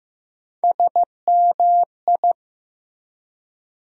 Morse code, heard as a keyed tone, about 15 wpm, 700 Hz. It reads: SMI
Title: USA Call Sign Postfixes - 3 Letters 15wpm